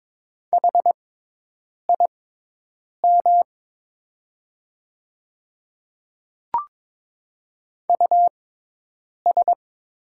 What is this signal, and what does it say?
Morse code, heard as a keyed tone, about 22 wpm, spaced as for 8 wpm, 700 Hz. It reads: HIM US
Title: 6x Character Spacing - Top 500 Words - 22wpm